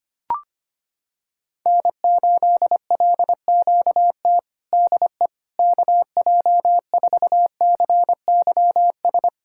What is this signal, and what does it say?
Morse code, heard as a keyed tone, about 25 wpm, 700 Hz. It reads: N8LQT DE KJ4CYS
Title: K1USN SST QSO Practice 25wpm